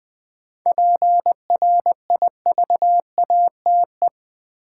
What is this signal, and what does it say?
Morse code, heard as a keyed tone, about 20 wpm, 700 Hz. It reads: PRIVATE
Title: Sentences from Top 1000 Words - New Word Drill - 20wpm